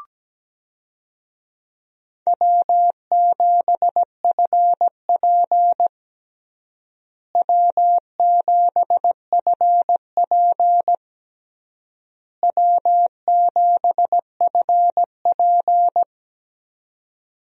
Morse code, heard as a keyed tone, about 17 wpm, 700 Hz. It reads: W7FP W7FP W7FP
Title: USA Call Signs - 4 Characters - Repeated 3x - 17wpm